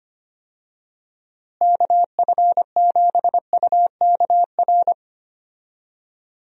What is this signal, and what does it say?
Morse code, heard as a keyed tone, about 25 wpm, 700 Hz. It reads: KF7UKR
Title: USA Call Signs - 6 Characters - Repeated 2x - 25wpm